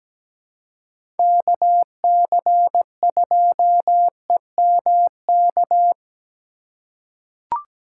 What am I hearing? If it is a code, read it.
Morse code, heard as a keyed tone, about 17 wpm, 700 Hz. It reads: KC2EMK
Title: USA Call Signs - 6 Characters - 3x Character Spacing - 17wpm